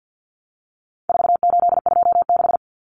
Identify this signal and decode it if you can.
Morse code, heard as a keyed tone, about 50 wpm, 700 Hz. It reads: 4826
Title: Numbers - 4-Digit 50wpm